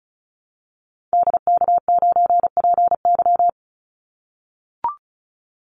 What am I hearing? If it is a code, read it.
Morse code, heard as a keyed tone, about 34 wpm, 700 Hz. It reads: DK9PY
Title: Call Signs - 35wpm